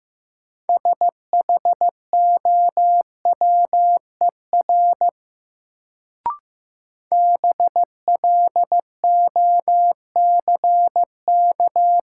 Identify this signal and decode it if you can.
Morse code, heard as a keyed tone, about 15 wpm, 700 Hz. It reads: SHOWER BLOCK
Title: Sentences from Top 1600 Words - New Word Drill Alt - 15wpm